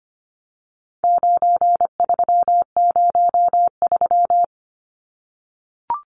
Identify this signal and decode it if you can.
Morse code, heard as a keyed tone, about 25 wpm, 700 Hz. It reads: 9303
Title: Numbers - 4-Digit 25wpm